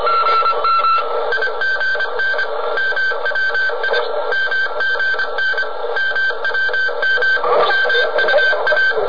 Morse code, heard as a keyed tone, about 25 wpm, 1550 Hz, with radio noise. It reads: IGN MWI MGN MWM MUA